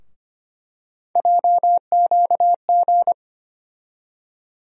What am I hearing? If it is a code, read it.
Morse code, heard as a keyed tone, about 25 wpm, 700 Hz. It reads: JQG
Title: USA Call Sign Postfixes - 3 Letters 25wpm